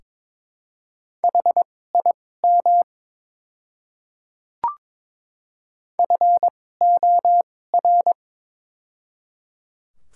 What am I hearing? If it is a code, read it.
Morse code, heard as a keyed tone, about 22 wpm, spaced as for 16 wpm, 700 Hz. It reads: HIM FOR
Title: 2x Character Spacing - Top 100 Words - 22wpm